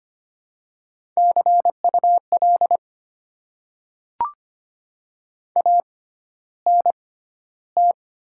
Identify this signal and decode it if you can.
Morse code, heard as a keyed tone, about 25 wpm, spaced as for 13 wpm, 700 Hz. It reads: CUL ANT
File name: Top 90 QSO Elements - 6x Character Spacing - 25wpm